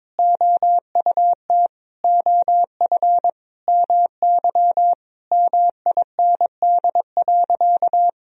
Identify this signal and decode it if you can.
Morse code, heard as a keyed tone, about 21 wpm, 700 Hz. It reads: OUT OF MY MIND.
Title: Sentences from Top 1000 Words - Review Part 4 - 22wpm